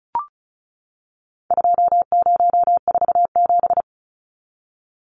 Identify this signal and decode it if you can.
Morse code, heard as a keyed tone, about 34 wpm, 700 Hz. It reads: 2047
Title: Numbers - 4-Digit 35wpm